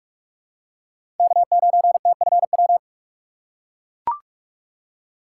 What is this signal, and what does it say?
Morse code, heard as a keyed tone, about 41 wpm, 700 Hz. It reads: K9TFW
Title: USA Call Signs - 5 Characters - 4x Character Spacing - 45wpm